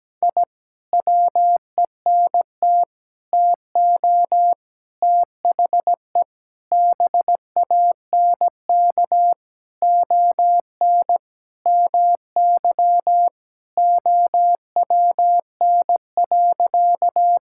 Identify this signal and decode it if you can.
Morse code, heard as a keyed tone, about 17 wpm, 700 Hz. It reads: I WENT TO THE BANK ON MY OWN.
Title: Sentences from Top 1000 Words - Review Part 6 - 17wpm